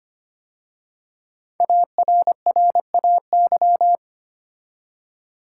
Morse code, heard as a keyed tone, about 25 wpm, 700 Hz. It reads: ARRAY
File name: 5 Letter Words - First 3 Letters Given - 25wpm